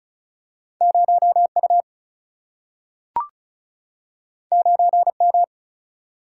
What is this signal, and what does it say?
Morse code, heard as a keyed tone, about 36 wpm, 700 Hz. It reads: 0U 9M
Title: Single-Letter Single-Number 35wpm